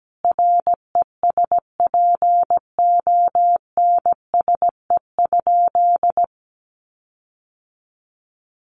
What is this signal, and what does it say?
Morse code, heard as a keyed tone, about 17 wpm, 700 Hz. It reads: RESPONSE?
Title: Sentences from Top 1400 Words 17wpm